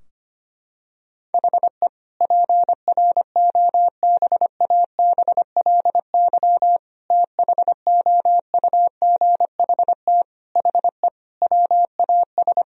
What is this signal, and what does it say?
Morse code, heard as a keyed tone, about 25 wpm, 700 Hz. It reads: HE PROBABLY THOUGHT HE WAS
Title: Sentences from Top 1000 Words - Review Part 4 - 25wpm